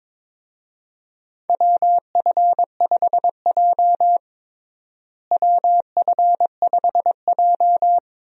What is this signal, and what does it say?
Morse code, heard as a keyed tone, about 22 wpm, 700 Hz. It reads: WF5J WF5J
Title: USA Call Signs - 4 Characters - Repeated 2x - 22wpm